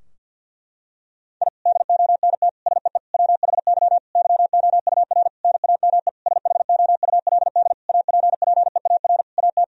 Morse code, heard as a keyed tone, about 50 wpm, 700 Hz. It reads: I DONT SEE WHY YOUR NAME SHOULD APPEAR AT
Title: Sentences from Top 1000 Words - Review Part 2 - 50wpm